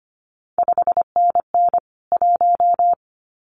Morse code, heard as a keyed tone, about 25 wpm, 700 Hz. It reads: 5NN 1
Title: CQ WW DX Contest Exchange Encore - 25wpm